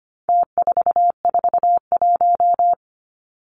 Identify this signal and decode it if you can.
Morse code, heard as a keyed tone, about 25 wpm, 700 Hz. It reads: T441
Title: Numbers - Contest Style - 4-Digit 25wpm